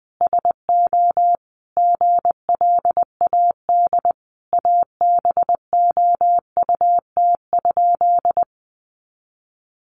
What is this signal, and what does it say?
Morse code, heard as a keyed tone, about 20 wpm, 700 Hz. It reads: SO GLAD ABOUT?
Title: Sentences from Top 800 Words 20wpm